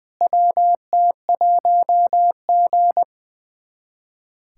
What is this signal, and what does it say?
Morse code, heard as a keyed tone, about 20 wpm, 700 Hz. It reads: WT1G
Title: USA Callsign DE Callsign - 4 Characters - 20wpm